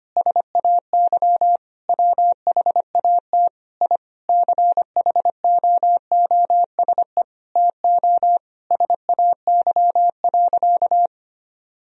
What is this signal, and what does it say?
Morse code, heard as a keyed tone, about 25 wpm, 700 Hz. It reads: SAY WHAT I CHOOSE TO SAY.
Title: Sentences from Top 1000 Words - Review Part 4 - 25wpm